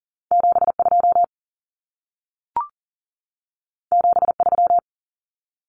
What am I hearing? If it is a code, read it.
Morse code, heard as a keyed tone, about 40 wpm, 700 Hz. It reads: 72 73
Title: Numbers - 2-Digit 40wpm